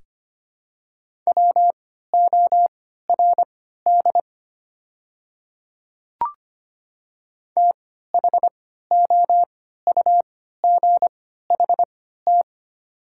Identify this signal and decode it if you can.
Morse code, heard as a keyed tone, about 25 wpm, spaced as for 14 wpm, 690 Hz. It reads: WORD THOUGHT
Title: 3x Character Spacing - Top 300 Words - 25wpm